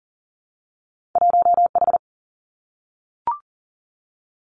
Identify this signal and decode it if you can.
Morse code, heard as a keyed tone, about 40 wpm, 700 Hz. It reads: 1 H
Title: Two Challenging Letter-Number Combinations 40wpm